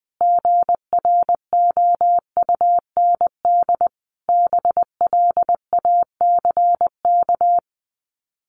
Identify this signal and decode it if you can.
Morse code, heard as a keyed tone, about 20 wpm, 700 Hz. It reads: GROUND BLACK
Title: Sets of 5 Words Encore 20wpm